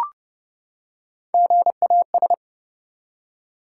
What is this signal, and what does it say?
Morse code, heard as a keyed tone, about 30 wpm, 700 Hz. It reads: GAS
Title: Three Letter Words - 30wpm